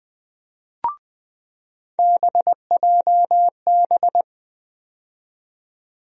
Morse code, heard as a keyed tone, about 20 wpm, 700 Hz. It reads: BJB
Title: USA Call Sign Postfixes - 3 Letters 20wpm